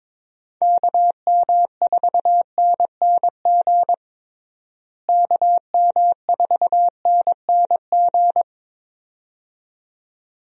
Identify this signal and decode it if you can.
Morse code, heard as a keyed tone, about 22 wpm, 700 Hz. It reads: KM4NNG KM4NNG
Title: USA Call Signs - 6 Characters - Repeated 2x - 22wpm